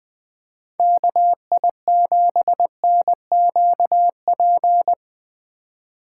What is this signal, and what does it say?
Morse code, heard as a keyed tone, about 23 wpm, 700 Hz. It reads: KI7NQP
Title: Speed Racing - USA 2x3 Call Signs 20wpm